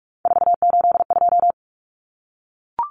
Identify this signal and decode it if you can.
Morse code, heard as a keyed tone, about 44 wpm, 700 Hz. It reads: 482
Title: Common Cut Numbers - 3-Digit 45wpm